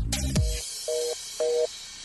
Reorder tone (North America), 480 + 620 Hz.